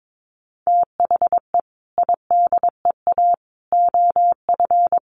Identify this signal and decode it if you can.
Morse code, heard as a keyed tone, about 22 wpm, 700 Hz. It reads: THE IDEA OF
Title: Sentences from Top 1100 Words 22wpm